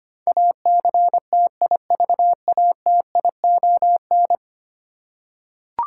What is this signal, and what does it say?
Morse code, heard as a keyed tone, about 25 wpm, 700 Hz. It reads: ACTIVATION
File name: 10 Letter Words - First 3 Letters Given - 25wpm